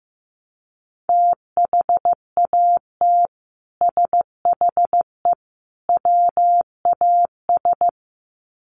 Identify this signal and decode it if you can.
Morse code, heard as a keyed tone, about 15 wpm, 700 Hz. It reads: THAT SHE WAS
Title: Sets of 3 Words 15wpm